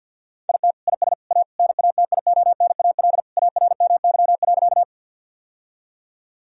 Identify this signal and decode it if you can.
Morse code, heard as a keyed tone, about 50 wpm, 700 Hz. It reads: IT IS A NATIONAL ARMY.
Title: Sentences from Top 1000 Words - Review Part 4 - 50wpm